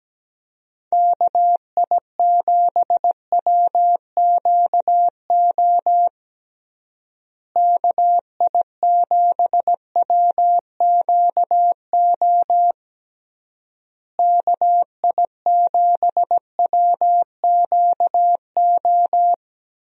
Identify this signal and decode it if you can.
Morse code, heard as a keyed tone, about 17 wpm, 700 Hz. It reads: KI7WQO KI7WQO KI7WQO
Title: USA Call Signs - 6 Characters - Repeated 3x - 17wpm